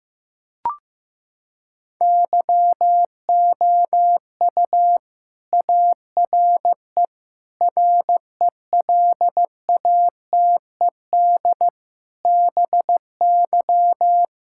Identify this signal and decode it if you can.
Morse code, heard as a keyed tone, about 15 wpm, 700 Hz. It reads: YOU ARE RELATED BY
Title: Sentences from Top 1000 Words - Review Part 5 - 15wpm